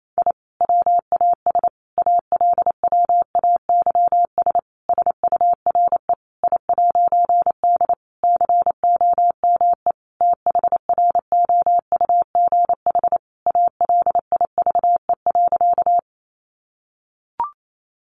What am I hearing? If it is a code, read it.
Morse code, heard as a keyed tone, about 28 wpm, 700 Hz. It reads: I WAS ALWAYS SURE I'D COME THROUGH ALIVE.
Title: A Sound Of Thunder 28wpm